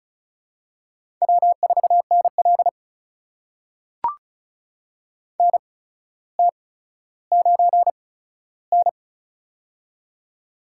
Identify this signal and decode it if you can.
Morse code, heard as a keyed tone, about 36 wpm, spaced as for 15 wpm, 700 Hz. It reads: W4NL NT9N
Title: USA Call Signs - 4 Characters - 8x Character Spacing - 35wpm